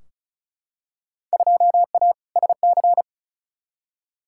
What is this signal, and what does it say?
Morse code, heard as a keyed tone, about 36 wpm, 700 Hz. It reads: 2A SC
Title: Field Day Exchanges - 35wpm